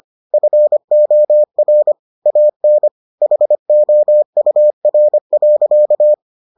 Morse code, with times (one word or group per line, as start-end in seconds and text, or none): 0.3-1.9 FOR
2.3-2.9 AN
3.2-6.1 HOUR.